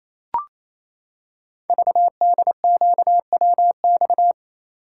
Morse code, heard as a keyed tone, about 28 wpm, 700 Hz. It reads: VDQWX
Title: Code Groups - 5 Letter:Numbers - 28wpm